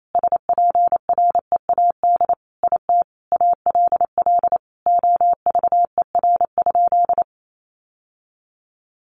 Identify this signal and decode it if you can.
Morse code, heard as a keyed tone, about 28 wpm, 700 Hz. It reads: SPREAD IT ALL OVER?
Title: Sentences from Top 1200 Words 28wpm